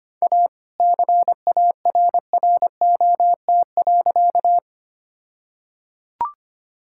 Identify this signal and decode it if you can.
Morse code, heard as a keyed tone, about 25 wpm, 700 Hz. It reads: A CARROT.
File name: practice / 25wpm / All-Age Appropriate Jokes - 25wpm